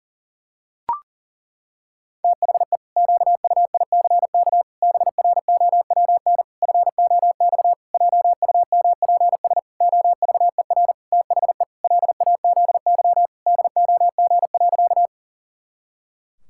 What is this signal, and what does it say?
Morse code, heard as a keyed tone, about 40 wpm, 700 Hz. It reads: THE QUICK BROWN FOX JUMPS OVER THE LAZY DOG.
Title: Pangram Challenge v2 - 40wpm